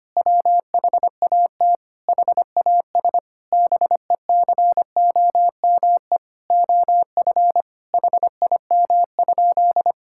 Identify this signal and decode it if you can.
Morse code, heard as a keyed tone, about 25 wpm, 700 Hz. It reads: WHAT HAS BECOME OF HIM?
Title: Sentences from Top 1000 Words - Review Part 5 - 25wpm